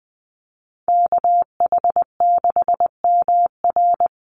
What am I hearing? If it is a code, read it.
Morse code, heard as a keyed tone, about 19 wpm, 700 Hz. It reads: KH6MR
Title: USA Call Signs 20wpm